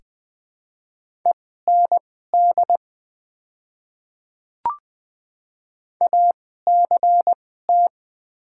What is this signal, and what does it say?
Morse code, heard as a keyed tone, about 20 wpm, spaced as for 14 wpm, 700 Hz. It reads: END ACT
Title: 2x Character Spacing - Top 500 Words - 20wpm